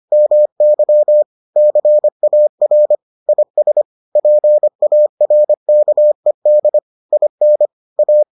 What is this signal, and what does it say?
Morse code, heard as a keyed tone, about 25 wpm, 600 Hz. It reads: MY CAR IS PARKED IN A